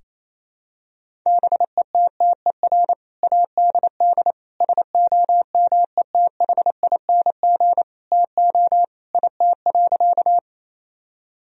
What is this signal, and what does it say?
Morse code, heard as a keyed tone, about 28 wpm, 700 Hz. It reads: BETTER ADD SOMETHING TO IT.
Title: Sentences from Top 1000 Words - Review Part 1 - 28wpm